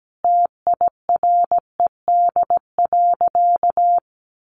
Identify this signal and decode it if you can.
Morse code, heard as a keyed tone, about 17 wpm, 700 Hz. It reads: TIRED.
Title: Easy Phrases 17wpm